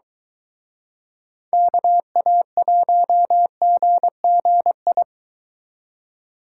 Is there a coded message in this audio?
Yes: Morse code from 1.5 to 5.0 s, heard as a keyed tone, about 23 wpm, 700 Hz. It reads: KA1GGI